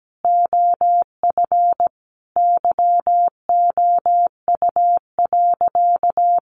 Morse code, heard as a keyed tone, about 17 wpm, 700 Hz. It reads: OF YOU.
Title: Sentences from Top 1400 Words 17wpm